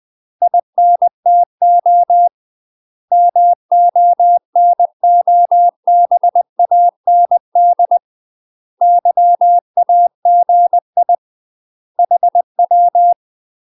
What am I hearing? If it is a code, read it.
Morse code, heard as a keyed tone, about 20 wpm, 700 Hz. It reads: INTO MONOBAND YAGI HW
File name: UK MASTER QSOs TRAINING / 20WPM / UK MASTER QSOs 0045